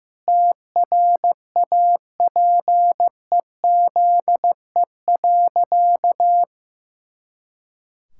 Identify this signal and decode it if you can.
Morse code, heard as a keyed tone, about 15 wpm, 700 Hz. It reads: TRAPEZE.
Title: Pangram Challenge v2 - 15wpm